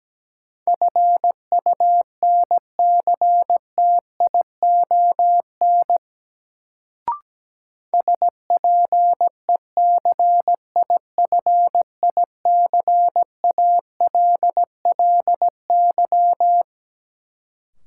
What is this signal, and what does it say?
Morse code, heard as a keyed tone, about 17 wpm, 700 Hz. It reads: FUNCTION SPECIFICALLY
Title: Sentences from Top 1100 Words - New Word Drill - 17wpm